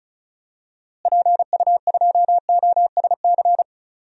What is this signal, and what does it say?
Morse code, heard as a keyed tone, about 34 wpm, 700 Hz. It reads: PU2OSC
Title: DXCC-100 Call Signs - 6 Characters - Repeated 2x - 35wpm